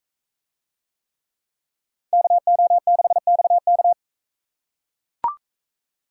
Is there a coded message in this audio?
Yes: Morse code from 2.1 to 3.9 s, heard as a keyed tone, about 42 wpm, 700 Hz. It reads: KO6XK